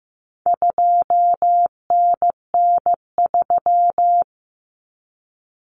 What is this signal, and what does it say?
Morse code, heard as a keyed tone, about 15 wpm, 700 Hz. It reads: 2NN3
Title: Common Cut Numbers - 4-Digit 15wpm